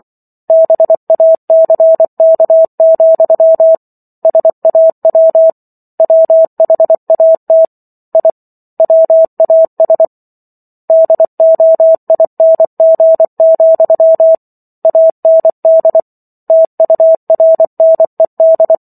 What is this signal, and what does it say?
Morse code, heard as a keyed tone, about 24 wpm, 650 Hz. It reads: BACK, SAW WHAT I WAS DOING, AND TURNED